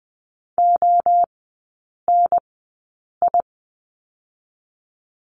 Morse code, heard as a keyed tone, about 20 wpm, 700 Hz. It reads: O N I
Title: 029 Lesson 6e - Letter i - Three Characters 20wpm